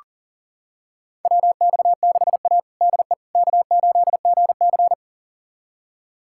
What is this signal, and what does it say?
Morse code, heard as a keyed tone, about 37 wpm, 700 Hz. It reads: WX6A DE K8GC
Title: USA Callsign DE Callsign - 4 Characters - 40wpm